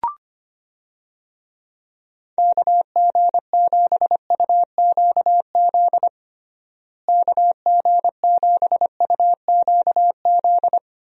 Morse code, heard as a keyed tone, about 25 wpm, 700 Hz. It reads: KG7UQZ KG7UQZ
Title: USA Call Signs - 6 Characters - Repeated 2x - 25wpm